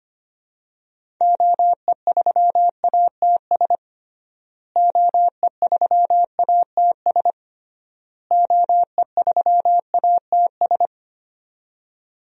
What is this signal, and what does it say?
Morse code, heard as a keyed tone, about 25 wpm, 700 Hz. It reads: OE3ATS OE3ATS OE3ATS
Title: DXCC-100 Call Signs - 6 Characters - Repeated 3x - 25wpm